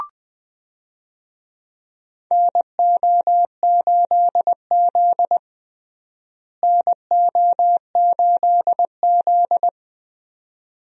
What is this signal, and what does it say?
Morse code, heard as a keyed tone, about 20 wpm, 700 Hz. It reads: NO8Z NO8Z
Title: USA Call Signs - 4 Characters - Repeated 2x - 20wpm